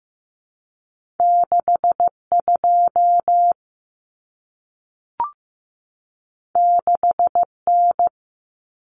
Morse code, heard as a keyed tone, about 15 wpm, 700 Hz. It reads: 62 6N
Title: Common Cut Numbers - 2-Digit 15wpm